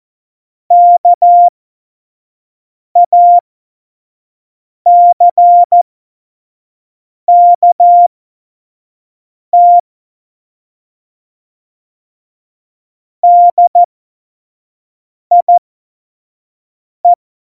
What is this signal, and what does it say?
Morse code, heard as a keyed tone, about 14 wpm, spaced as for 5 wpm, 700 Hz. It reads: KACKT DIE
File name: Morsekurs / Witze / Witz 18 Juli 17, Tempo 25 F70